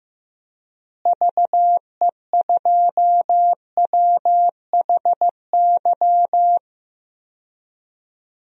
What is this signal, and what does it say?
Morse code, heard as a keyed tone, about 15 wpm, 700 Hz. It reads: VE2WHY
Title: DXCC-100 Call Signs - 6 Characters - Repeated 3x - 15wpm